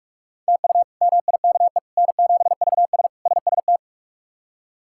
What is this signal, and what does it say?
Morse code, heard as a keyed tone, about 46 wpm, 700 Hz. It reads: TU MIKE N7VS SST
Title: K1USN SST QSO Practice - 45wpm